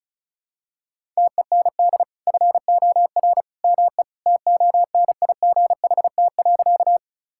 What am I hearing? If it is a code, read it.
Morse code, heard as a keyed tone, about 34 wpm, 700 Hz. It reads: TEND FOR ME TONIGHT.
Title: Sentences from Top 1000 Words - Review Part 5 - 35wpm